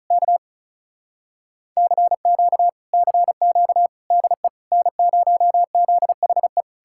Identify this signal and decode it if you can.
Morse code, heard as a keyed tone, about 47 wpm, spaced as for 41 wpm, 700 Hz. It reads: K CQ CQ DE N0ZHE
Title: Speed-Racing Common QSO Phrases - 35wpm